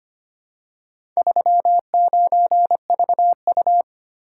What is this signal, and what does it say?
Morse code, heard as a keyed tone, about 25 wpm, 700 Hz. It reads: 39VU
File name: practice / 25wpm / Four Letter-Number Combinations - 25wpm